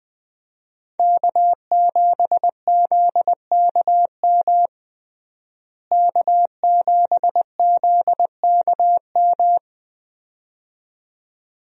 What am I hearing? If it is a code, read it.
Morse code, heard as a keyed tone, about 21 wpm, 700 Hz. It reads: K7ZKM K7ZKM
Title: USA Call Signs - 5 Characters - Repeated 2x - 20wpm